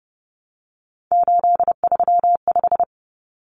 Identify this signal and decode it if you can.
Morse code, heard as a keyed tone, about 30 wpm, 700 Hz. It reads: 835
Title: Numbers - 3-Digit 30wpm